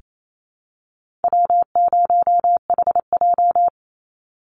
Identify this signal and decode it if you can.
Morse code, heard as a keyed tone, about 28 wpm, 700 Hz. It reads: W0HJ
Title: USA Call Signs 28wpm